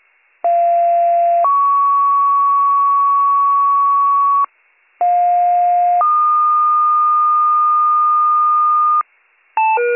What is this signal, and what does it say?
Two-tone page, heard as a one-second tone, then a three-second tone, about 690 then 1120 Hz, about 710 then 1220 Hz. It is heard from 0.4 to 9.0 s.